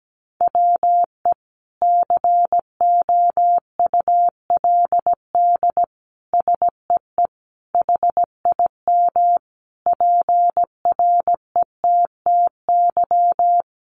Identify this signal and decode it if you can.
Morse code, heard as a keyed tone, about 17 wpm, 700 Hz. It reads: WE COULD SEE HIM PRETTY
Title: Sentences from Top 1000 Words 17wpm